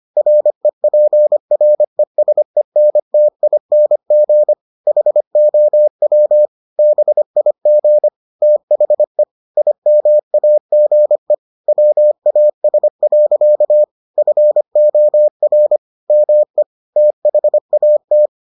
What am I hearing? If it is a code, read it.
Morse code, heard as a keyed tone, about 25 wpm, 600 Hz. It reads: REPRESENTING HOW BIG THE IMAGE WAS. FOR ME THAT